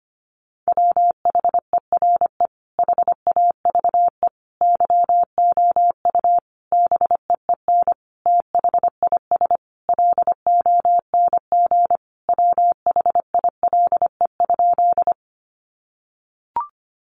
Morse code, heard as a keyed tone, about 25 wpm, 700 Hz. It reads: WHERE HAVE YOU BEEN THIS LONG WHILE?
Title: Sentences from Top 300 Words 25wpm